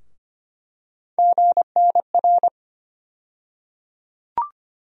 Morse code, heard as a keyed tone, about 25 wpm, 700 Hz. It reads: GNR
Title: USA Call Sign Postfixes - 3 Letters 25wpm